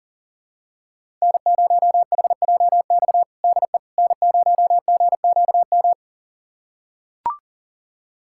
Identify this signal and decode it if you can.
Morse code, heard as a keyed tone, about 40 wpm, 700 Hz. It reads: N0HJX DE N0GQM
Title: USA Callsign DE Callsign - 5 Characters - 40wpm